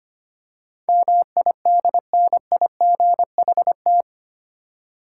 Morse code, heard as a keyed tone, about 25 wpm, 700 Hz. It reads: MIDNIGHT
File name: 8 Letter Words - First 3 Letters Given - 25wpm